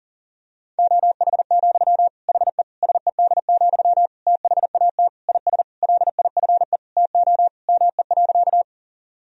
Morse code, heard as a keyed tone, about 37 wpm, 700 Hz. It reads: OH, HE SED, THAT IS LIFE TO ME.
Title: Sentences from Top 500 Words and Abbreviations 40wpm